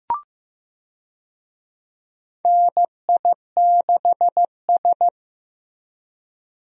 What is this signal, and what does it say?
Morse code, heard as a keyed tone, about 15 wpm, 700 Hz. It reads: NI6S
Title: USA Call Signs - 4 Characters - Repeated 3x - 15wpm